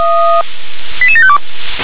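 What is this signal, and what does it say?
Five-tone selective call (ZVEI), heard as quick runs of five tones, tones of about 70 ms. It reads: 88752
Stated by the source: ZVEI five-tone call 88712 received with an SDR.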